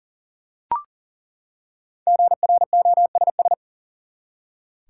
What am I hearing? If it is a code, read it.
Morse code, heard as a keyed tone, about 40 wpm, 700 Hz. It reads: GROSS
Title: Five Letter Words - 40wpm